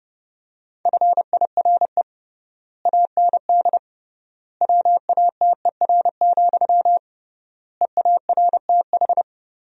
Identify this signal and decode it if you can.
Morse code, heard as a keyed tone, about 30 wpm, 700 Hz. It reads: FIRE AND WATER, EARTH
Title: 3x Word-Spacing - Sentences from Top 400 Words - 30wpm